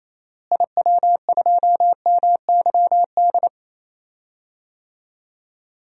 Morse code, heard as a keyed tone, about 27 wpm, 700 Hz. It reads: IW2MYD
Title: DXCC-100 Call Signs - 6 Characters - Repeated 2x - 28wpm